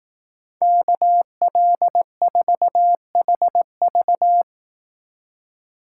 Morse code, heard as a keyed tone, about 18 wpm, 700 Hz. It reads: KL4HV
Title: Speed Racing - USA 2x2 Call Signs 15wpm